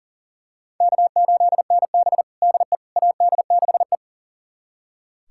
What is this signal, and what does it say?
Morse code, heard as a keyed tone, about 40 wpm, 700 Hz. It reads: K8NB DE AD6E
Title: USA Callsign DE Callsign - 4 Characters - 40wpm